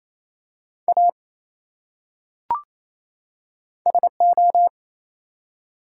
Morse code, heard as a keyed tone, about 28 wpm, 700 Hz. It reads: A SO
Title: Top 180 QSO Elements 28wpm